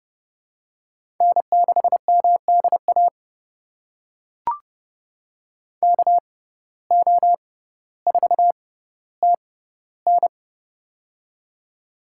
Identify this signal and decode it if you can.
Morse code, heard as a keyed tone, about 30 wpm, spaced as for 15 wpm, 700 Hz. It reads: N6MDA KO4TN
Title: USA Call Signs - 6x Character Spacing - 30wpm